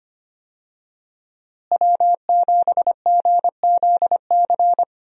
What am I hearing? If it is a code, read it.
Morse code, heard as a keyed tone, about 25 wpm, 700 Hz. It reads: W7GZC
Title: USA Call Signs - 5 Characters - Repeated 2x - 25wpm